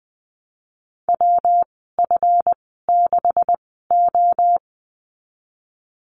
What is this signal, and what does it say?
Morse code, heard as a keyed tone, about 20 wpm, spaced as for 15 wpm, 700 Hz. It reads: WF6O
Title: 147 Lesson 24g - Letter y - Call Signs 20wpm